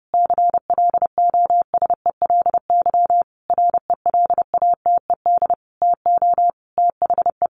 Morse code, heard as a keyed tone, about 30 wpm, 700 Hz. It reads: CLOSELY RELATED TO THE
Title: Sentences from Top 1400 Words 30wpm